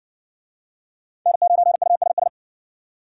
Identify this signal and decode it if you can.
Morse code, heard as a keyed tone, about 57 wpm, 700 Hz. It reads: N9USS